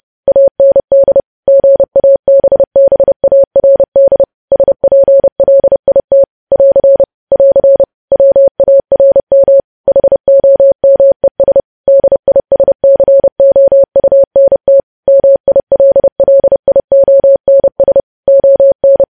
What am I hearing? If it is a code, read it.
Morse code, heard as a keyed tone, about 30 wpm, 550 Hz. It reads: AND GABBARD SPLIT <AR> <AR> WARM HOMES DISCOUNT MILLIONS ON
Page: news headlines in morse code at 30 wpm